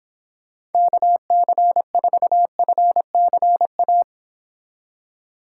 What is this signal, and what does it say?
Morse code, heard as a keyed tone, about 29 wpm, 700 Hz. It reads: KC4FCA